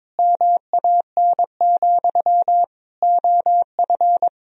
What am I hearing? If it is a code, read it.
Morse code, heard as a keyed tone, about 22 wpm, 700 Hz. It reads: MAN, OF
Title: Sentences from Top 1000 Words - Review Part 1 - 22wpm